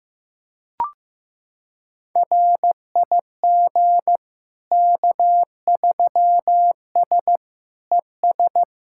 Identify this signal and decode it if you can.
Morse code, heard as a keyed tone, about 15 wpm, 700 Hz. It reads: RIG K3S ES
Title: Practice QSO - Exchange 2 - Info - 15wpm